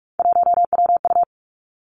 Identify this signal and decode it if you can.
Morse code, heard as a keyed tone, about 46 wpm, 700 Hz. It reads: 1WU
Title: Three Challenging Letter-Number Combinations 45wpm